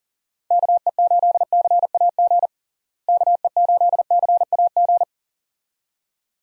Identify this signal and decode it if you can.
Morse code, heard as a keyed tone, about 37 wpm, 700 Hz. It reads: KE8CAG KE8CAG
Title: USA Call Signs - 6 Characters - Repeated 2x - 40wpm